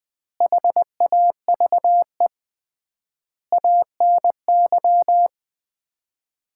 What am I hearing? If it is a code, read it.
Morse code, heard as a keyed tone, about 21 wpm, 700 Hz. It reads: HAVE ANY
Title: 3x Word-Spacing - Sets of 5 Words - 20wpm